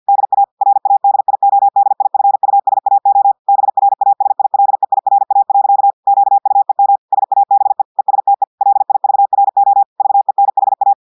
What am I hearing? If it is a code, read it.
Morse code, heard as a keyed tone, about 50 wpm, 800 Hz. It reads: DA RADIODIFUSAO BRASILEIRA. QUEM SABE ESTE LIVRO VENHA
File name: audios / portugais / portugais texte en clair 1 vitesse 50 mots minutes